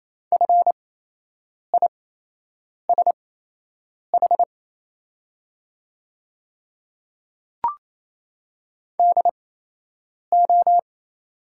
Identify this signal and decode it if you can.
Morse code, heard as a keyed tone, about 27 wpm, spaced as for 8 wpm, 690 Hz. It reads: FISH DO
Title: 8x Character Spacing - Top 300 Words - 28wpm